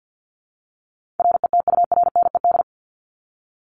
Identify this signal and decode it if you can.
Morse code, heard as a keyed tone, about 50 wpm, 700 Hz. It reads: RETURNED
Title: ED Suffix - Top 100 Words with Base Words 50wpm